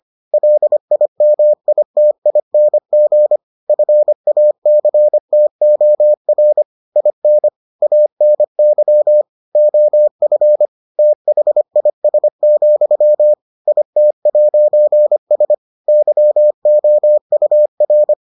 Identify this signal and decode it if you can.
Morse code, heard as a keyed tone, about 25 wpm, 600 Hz. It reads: LIMITING FACTOR IN ANY OF THIS, IT'S YOUR